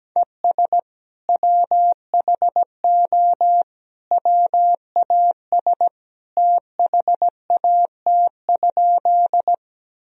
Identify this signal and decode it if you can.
Morse code, heard as a keyed tone, about 17 wpm, 700 Hz. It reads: ES WHO WAS THAT?
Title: Sentences from Top 100 Words and Abbreviations 17wpm